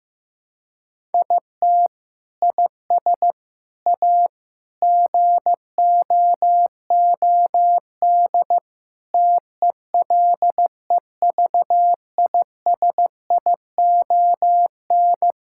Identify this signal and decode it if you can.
Morse code, heard as a keyed tone, about 16 wpm, 700 Hz. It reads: IT IS A GOOD TELEVISION
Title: Sentences from Top 1000 Words - Review Part 6 - 15wpm